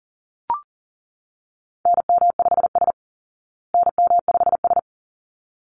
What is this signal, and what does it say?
Morse code, heard as a keyed tone, about 40 wpm, 700 Hz. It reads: NM5S NM5S
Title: Call Signs Repeated 3x - 40wpm